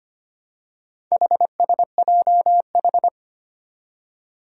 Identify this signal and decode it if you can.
Morse code, heard as a keyed tone, about 25 wpm, 700 Hz. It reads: HSJH
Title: Four Challenging Letter-Number Combinations 25wpm